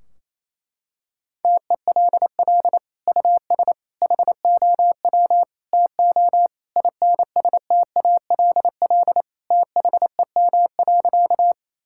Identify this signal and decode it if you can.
Morse code, heard as a keyed tone, about 27 wpm, 700 Hz. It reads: TELL US HOW TO INSTALL THEM.
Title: Sentences from Top 1600 Words 28wpm